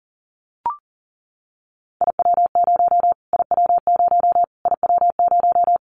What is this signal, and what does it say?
Morse code, heard as a keyed tone, about 40 wpm, 700 Hz. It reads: IW0 IW0 IW0
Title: Callsign Prefixes for Top 25 DXCC - Repeated 3x - 40wpm